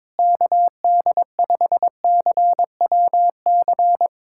Morse code, heard as a keyed tone, about 22 wpm, 700 Hz. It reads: KD5CWC
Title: USA Callsign DE Callsign 22wpm